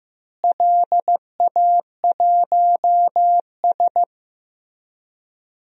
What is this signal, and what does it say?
Morse code, heard as a keyed tone, about 15 wpm, 700 Hz. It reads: LA1S
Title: DXCC-100 Call Signs - 4 Characters - Repeated 2x - 15wpm